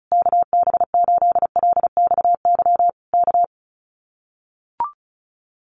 Morse code, heard as a keyed tone, about 34 wpm, 700 Hz. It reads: KB8LXY K
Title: Common QSO Phrases Encore 35wpm